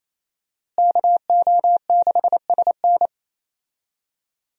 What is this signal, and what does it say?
Morse code, heard as a keyed tone, about 28 wpm, 700 Hz. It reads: KO6SN
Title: USA Call Signs - 5 Characters - Repeated 3x - 28wpm